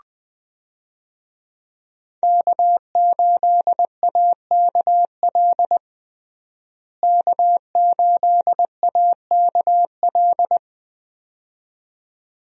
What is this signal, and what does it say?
Morse code, heard as a keyed tone, about 20 wpm, 700 Hz. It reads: K8AKL K8AKL
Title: USA Call Signs - 5 Characters - Repeated 2x - 20wpm